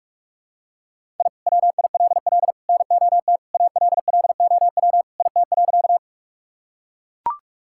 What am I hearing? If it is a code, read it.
Morse code, heard as a keyed tone, about 41 wpm, 700 Hz. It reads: I WILL NOT ALLOW IT.
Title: Sentences from Top 1000 Words - Review Part 5 - 45wpm